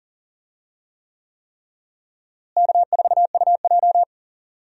Morse code, heard as a keyed tone, about 40 wpm, 700 Hz. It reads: K4UJ
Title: USA Call Signs - 4 Characters - Repeated 3x - 40wpm